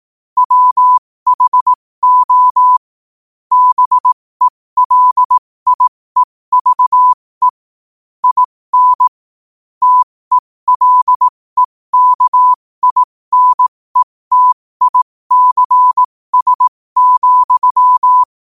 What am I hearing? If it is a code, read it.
Morse code, heard as a keyed tone, about 18 wpm, spaced as for 15 wpm, 1000 Hz. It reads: WHO BELIEVE IN TELEKINETICS,